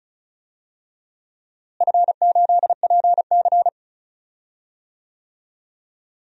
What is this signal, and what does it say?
Morse code, heard as a keyed tone, about 36 wpm, 700 Hz. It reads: F8PC
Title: DXCC-100 Call Signs - 4 Characters - Repeated 3x - 35wpm